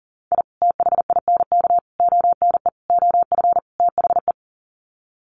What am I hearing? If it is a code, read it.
Morse code, heard as a keyed tone, about 39 wpm, 700 Hz. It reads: I THINK ONE OF THE
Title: Sets of 5 Words 40wpm